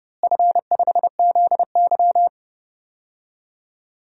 Morse code, heard as a keyed tone, about 30 wpm, 700 Hz. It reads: F5ZY
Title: DXCC-100 Call Signs - 4 Characters - Repeated 2x - 30wpm